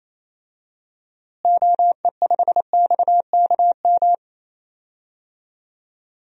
Morse code, heard as a keyed tone, about 28 wpm, 700 Hz. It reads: OE5XKM
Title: DXCC-100 Call Signs - 6 Characters - Repeated 3x - 28wpm